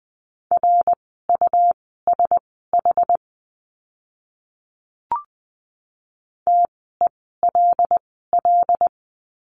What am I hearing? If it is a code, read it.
Morse code, heard as a keyed tone, about 20 wpm, spaced as for 14 wpm, 700 Hz. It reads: RUSH TELL
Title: 101 Lesson 17f - Letter u - Words 20wpm